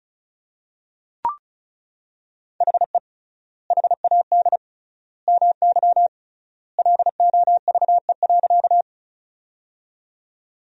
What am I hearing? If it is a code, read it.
Morse code, heard as a keyed tone, about 34 wpm, 700 Hz. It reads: HE HAD MY LOVE.
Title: 3x Word-Spacing - Sentences from Top 400 Words - 35wpm